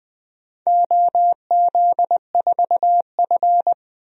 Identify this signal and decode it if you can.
Morse code, heard as a keyed tone, about 20 wpm, 700 Hz. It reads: OZ4F
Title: DXCC-100 Call Signs - 4 Characters - 20wpm